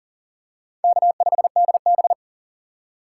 Morse code, heard as a keyed tone, about 38 wpm, 690 Hz. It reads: K5DB
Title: USA Call Signs - 4 Characters - 6x Character Spacing - 40wpm